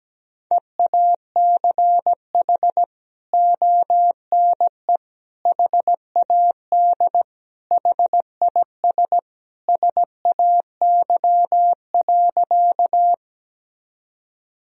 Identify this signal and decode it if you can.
Morse code, heard as a keyed tone, about 17 wpm, 700 Hz. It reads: EACH ONE HAD HIS SAY.
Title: Sentences from Top 1000 Words - Review Part 1 - 17wpm